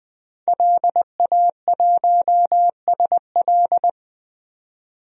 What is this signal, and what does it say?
Morse code, heard as a keyed tone, about 21 wpm, 700 Hz. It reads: LA1SL
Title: DXCC-100 Call Signs - 5 Characters - 20wpm